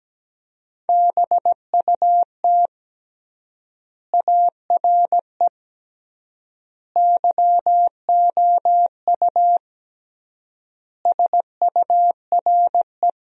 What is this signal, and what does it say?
Morse code, heard as a keyed tone, about 17 wpm, 700 Hz. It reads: BUT ARE YOU SURE
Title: 3x Word-Spacing - Sentences from Top 300 Words - 17wpm